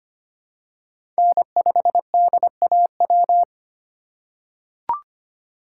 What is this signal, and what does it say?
Morse code, heard as a keyed tone, about 25 wpm, 700 Hz. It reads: N5DAW
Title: USA Call Signs - 5 Characters - 2x Character Spacing - 25wpm